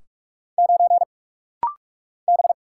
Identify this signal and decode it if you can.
Morse code, heard as a keyed tone, about 46 wpm, 690 Hz. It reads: 9 B
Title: Single Letter-Number - Rapid-Fire 45wpm